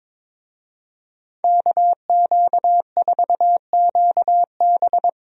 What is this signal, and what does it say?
Morse code, heard as a keyed tone, about 22 wpm, 700 Hz. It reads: KQ4QB
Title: USA Call Signs - 5 Characters - Repeated 2x - 22wpm